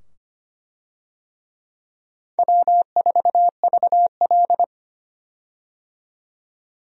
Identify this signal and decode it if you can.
Morse code, heard as a keyed tone, about 25 wpm, 700 Hz. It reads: W4VL
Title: USA Call Signs - 4 Characters - 25wpm